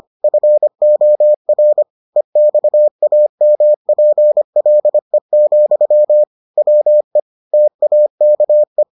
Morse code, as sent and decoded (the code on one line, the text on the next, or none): ..-. --- .-. / . -..- .- -- .--. .-.. --..-- / .-- . / - .- -.- .
FOR EXAMPLE, WE TAKE